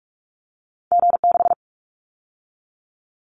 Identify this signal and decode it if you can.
Morse code, heard as a keyed tone, about 44 wpm, 700 Hz. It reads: G6
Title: Callsign Prefixes for Top 25 DXCC - 45wpm